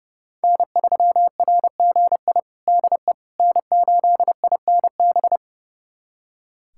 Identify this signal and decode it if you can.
Morse code, heard as a keyed tone, about 30 wpm, 700 Hz. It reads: N3RGI DE N8INB
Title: USA Callsign DE Callsign - 5 Characters - 30wpm